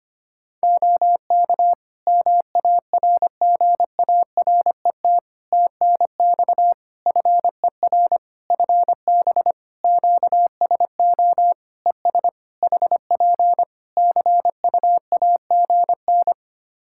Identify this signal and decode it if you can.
Morse code, heard as a keyed tone, about 25 wpm, 700 Hz. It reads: OK MARGARET TNX FER FB QSO ES HP CUAGN
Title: Practice QSO - Exchange 4 - Ending - 25wpm